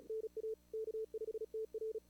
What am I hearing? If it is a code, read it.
Morse code, heard as a keyed tone, about 36 wpm, 440 Hz. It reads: NA K5TR
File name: CWSprint / Feb2022 / k5tr-feb-na-sprint-cw-k5ot -0000